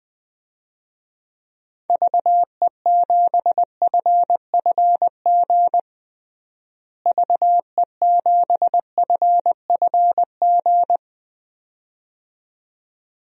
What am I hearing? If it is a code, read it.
Morse code, heard as a keyed tone, about 20 wpm, 700 Hz. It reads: VE7FFG VE7FFG
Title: DXCC-100 Call Signs - 6 Characters - Repeated 2x - 20wpm